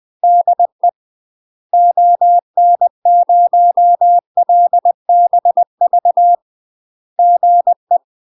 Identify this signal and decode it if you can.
Morse code, heard as a keyed tone, about 20 wpm, 700 Hz. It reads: DE ON0LBV GE
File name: UK MASTER QSOs TRAINING/20WPM